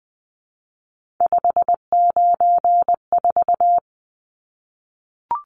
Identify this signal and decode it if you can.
Morse code, heard as a keyed tone, about 20 wpm, 700 Hz. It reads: 594
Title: Numbers - 3-Digit 20wpm